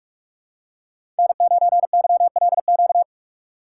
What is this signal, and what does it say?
Morse code, heard as a keyed tone, about 46 wpm, 700 Hz. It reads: N9YLQ
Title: USA Call Signs - 6x Character Spacing - 45wpm